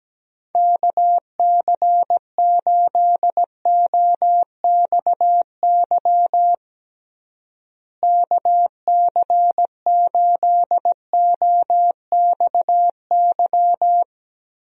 Morse code, heard as a keyed tone, about 17 wpm, 700 Hz. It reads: KC8OXY KC8OXY
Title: USA Call Signs - 6 Characters - Repeated 3x - 17wpm